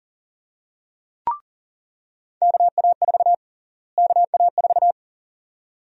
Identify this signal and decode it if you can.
Morse code, heard as a keyed tone, about 40 wpm, 700 Hz. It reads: KA4 KA4
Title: USA Call Sign Prefixes - Repeated 2x - 40wpm